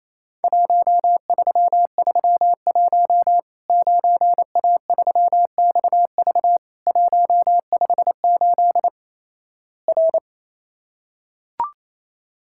Morse code, heard as a keyed tone, about 28 wpm, 700 Hz. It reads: 1331 9A3XV 158 R
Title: Worked All Europe DX Contest - QTC Exchange - 28wpm